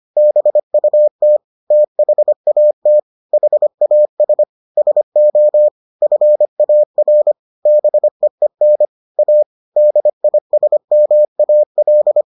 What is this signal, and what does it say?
Morse code, heard as a keyed tone, about 25 wpm, 600 Hz. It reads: BUT THAT HAS SO FAR BEEN A DISMAL